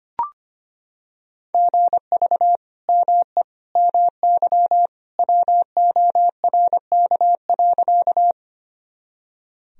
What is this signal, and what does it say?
Morse code, heard as a keyed tone, about 25 wpm, 700 Hz. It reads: GV ME MY WORK.
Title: Sentences from Top 200 Words and Abbreviations 25wpm